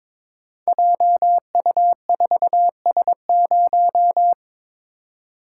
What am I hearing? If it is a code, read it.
Morse code, heard as a keyed tone, about 22 wpm, 700 Hz. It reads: JU4S0
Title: Code Groups - 5 Letter:Numbers - 22wpm